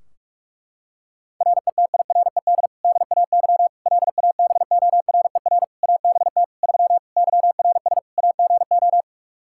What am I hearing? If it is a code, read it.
Morse code, heard as a keyed tone, about 46 wpm, 700 Hz. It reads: RETIRED DAY LABORER ABT 3 YRS AGO
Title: Practice QSO - Exchange 3 - More Info - 45wpm